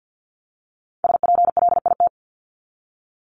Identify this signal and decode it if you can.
Morse code, heard as a keyed tone, about 50 wpm, 700 Hz. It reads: SPLIT
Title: Top 1751-2000 Words 50wpm